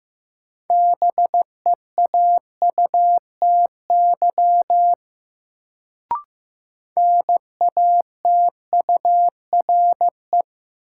Morse code, heard as a keyed tone, about 15 wpm, 700 Hz. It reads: BEAUTY NATURE
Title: Sentences from Top 600 Words - New Word Drill - 15wpm